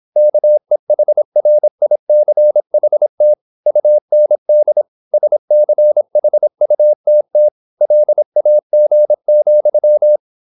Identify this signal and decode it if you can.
Morse code, heard as a keyed tone, about 26 wpm, 600 Hz. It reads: KEHRICHT UND SCHUTT LAG,